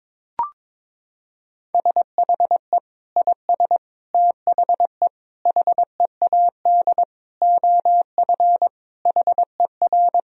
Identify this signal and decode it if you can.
Morse code, heard as a keyed tone, about 22 wpm, 700 Hz. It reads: SHE IS THE HEAD OF HER
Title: Sentences from Top 1000 Words - Review Part 2 - 22wpm